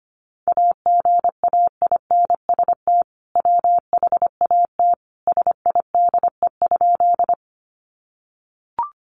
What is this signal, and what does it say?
Morse code, heard as a keyed tone, about 25 wpm, 700 Hz. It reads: AGAINST WHAT SIDE?
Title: Sentences from Top 400 Words 25wpm